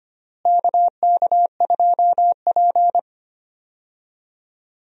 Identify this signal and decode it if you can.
Morse code, heard as a keyed tone, about 25 wpm, 700 Hz. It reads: KK2P
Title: USA Call Signs - 4 Characters - Repeated 3x - 25wpm